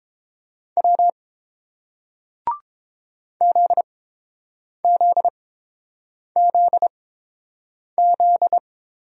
Morse code, heard as a keyed tone, about 30 wpm, spaced as for 7 wpm, 690 Hz. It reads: W ZZZZ